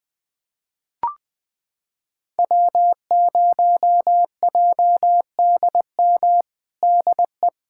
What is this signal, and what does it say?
Morse code, heard as a keyed tone, about 20 wpm, 700 Hz. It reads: W0JDM DE
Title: USA Callsign DE Callsign 20wpm